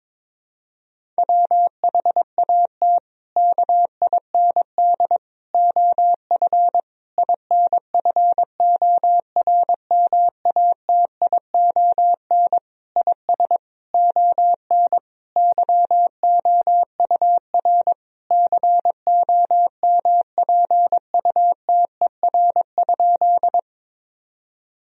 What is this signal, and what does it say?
Morse code, heard as a keyed tone, about 22 wpm, 700 Hz. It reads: WHAT KIND OF INFORMATION IS ON YOUR COMPUTER?
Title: Sentences from Top 1000 Words - Review Part 3 - 22wpm